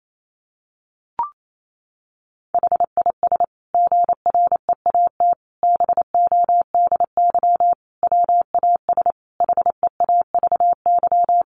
Morse code, heard as a keyed tone, about 28 wpm, 700 Hz. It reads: HIS GREAT BODY WAS HEAVY
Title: Sentences from Top 1400 Words 28wpm